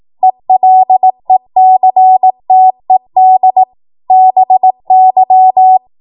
Morse code, heard as a keyed tone, about 18 wpm, 750 Hz. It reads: ELECTED BY
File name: Morse / 18WPM